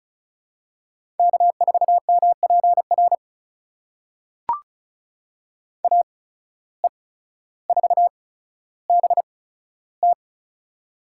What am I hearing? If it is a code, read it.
Morse code, heard as a keyed tone, about 36 wpm, spaced as for 15 wpm, 700 Hz. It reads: K4MPR AE4BT
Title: USA Call Signs - 5 Characters - 8x Character Spacing - 35wpm